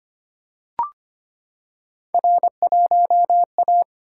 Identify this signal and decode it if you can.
Morse code, heard as a keyed tone, about 25 wpm, 700 Hz. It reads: R1A
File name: Three Letter-Number Combinations - 25wpm